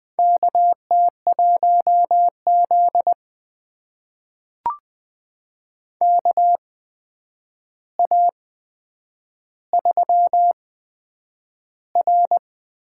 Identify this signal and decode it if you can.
Morse code, heard as a keyed tone, about 20 wpm, spaced as for 9 wpm, 700 Hz. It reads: KT1Z KA3R
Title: USA Call Signs - 4 Characters - 8x Character Spacing - 20wpm